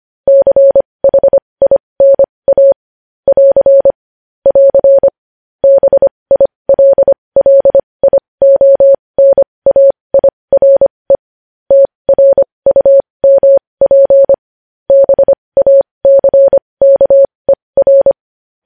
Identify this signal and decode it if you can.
Morse code, heard as a keyed tone, about 25 wpm, spaced as for 20 wpm, 550 Hz. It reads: CHINA <AR> <AR> BILLIONAIRE TRUMP BACKER
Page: news headlines in morse code at 20 wpm